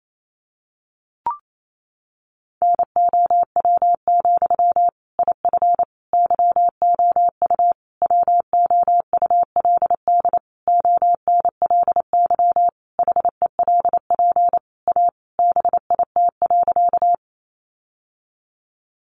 Morse code, heard as a keyed tone, about 28 wpm, 700 Hz. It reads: NOW, IF YOU WOULD ONLY HELP A BIT.
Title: Sentences from Top 900 Words 28wpm